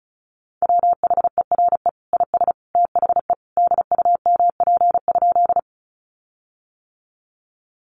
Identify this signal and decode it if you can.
Morse code, heard as a keyed tone, about 34 wpm, 700 Hz. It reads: WHERE IS THE DUMP?
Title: Sentences from Top 1300 Words 35wpm